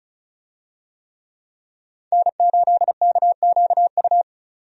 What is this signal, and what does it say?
Morse code, heard as a keyed tone, about 34 wpm, 700 Hz. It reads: N8KQU
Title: USA Call Signs - 5 Characters - 35wpm